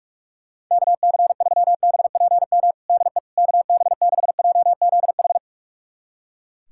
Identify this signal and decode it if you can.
Morse code, heard as a keyed tone, about 45 wpm, 700 Hz. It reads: KC3BPM DE KB6JZH
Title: USA Callsign DE Callsign - 6 Characters - 45wpm